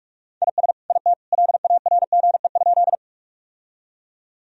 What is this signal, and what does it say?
Morse code, heard as a keyed tone, about 46 wpm, 700 Hz. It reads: IS IT LARGE?
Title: Sentences from Top 1000 Words - Review Part 1 - 45wpm